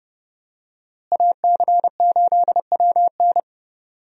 Morse code, heard as a keyed tone, about 31 wpm, 700 Hz. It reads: AC8WN
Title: USA Call Signs - 5 Characters - Repeated 2x - 30wpm